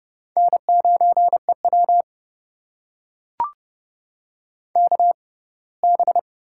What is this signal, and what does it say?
Morse code, heard as a keyed tone, about 30 wpm, spaced as for 20 wpm, 700 Hz. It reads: N9EW KB
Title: USA Call Signs - 6x Character Spacing - 30wpm